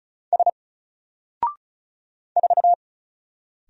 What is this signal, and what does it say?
Morse code, heard as a keyed tone, about 36 wpm, 700 Hz. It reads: S 4
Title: Single Character (Letter:Number:Punctuation:Prosign) Alt 35wpm